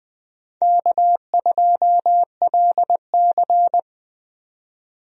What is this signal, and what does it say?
Morse code, heard as a keyed tone, about 20 wpm, 700 Hz. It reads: K2LC
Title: USA Call Signs - 4 Characters - 20wpm